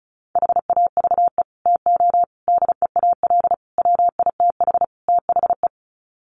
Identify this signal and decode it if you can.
Morse code, heard as a keyed tone, about 34 wpm, 700 Hz. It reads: HAVE TO DEAL WITH THE
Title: Sets of 5 Words Encore2 35wpm